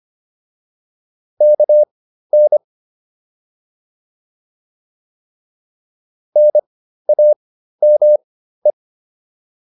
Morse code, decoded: KN NAME

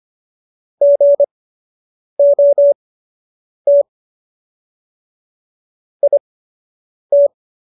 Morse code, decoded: GOT IT

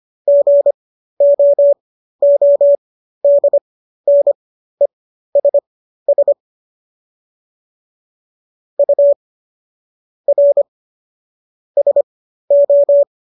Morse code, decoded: GOODNESS U R SO

